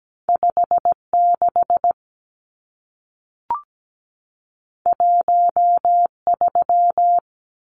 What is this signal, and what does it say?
Morse code, heard as a keyed tone, about 17 wpm, 700 Hz. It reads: 56 13
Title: Numbers - 2-Digit 17wpm